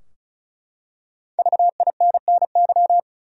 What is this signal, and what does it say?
Morse code, heard as a keyed tone, about 36 wpm, 700 Hz. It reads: VINNY
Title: USA Common Names - Max 5 Letters - 35wpm